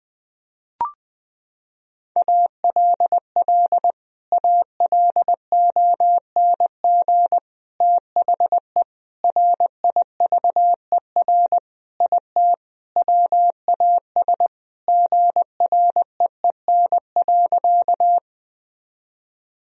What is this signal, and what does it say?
Morse code, heard as a keyed tone, about 20 wpm, 700 Hz. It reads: ALL ALONG THE RIVER IT WAS GREEN.
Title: Sentences from Top 1000 Words - Review Part 2 - 20wpm